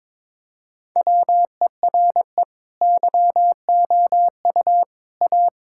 Morse code, heard as a keyed tone, about 22 wpm, 700 Hz. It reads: WERE YOU A
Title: Sentences from Top 1000 Words - Review Part 6 - 22wpm